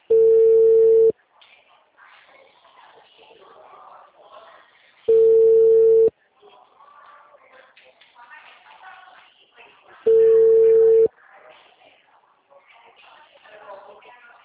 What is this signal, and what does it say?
Ringback tone (China, 450 Hz), with speech.
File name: out- -2002- - - .50146